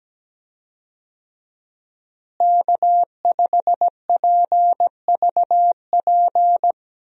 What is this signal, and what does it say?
Morse code, heard as a keyed tone, about 17 wpm, 700 Hz. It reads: K5PVP
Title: USA Call Signs - 5 Characters - Repeated 3x - 17wpm